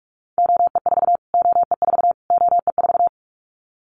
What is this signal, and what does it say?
Morse code, heard as a keyed tone, about 46 wpm, 700 Hz. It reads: OE4 OE4 OE4
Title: Callsign Prefixes for Top 25 DXCC - Repeated 3x - 45wpm